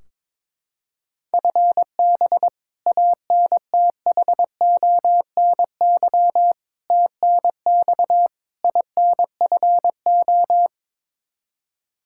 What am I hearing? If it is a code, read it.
Morse code, heard as a keyed tone, about 22 wpm, 700 Hz. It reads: FB ANTHONY TNX INFO
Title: K1USN SST QSO Practice 22wpm